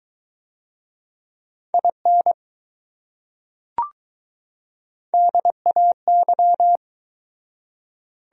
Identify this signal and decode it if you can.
Morse code, heard as a keyed tone, about 20 wpm, 700 Hz. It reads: IN DAY